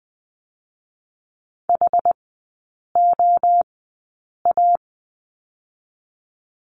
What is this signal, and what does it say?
Morse code, heard as a keyed tone, about 19 wpm, 700 Hz. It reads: H O A
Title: 218 Lesson 35e - Number 0 - Three Characters 20wpm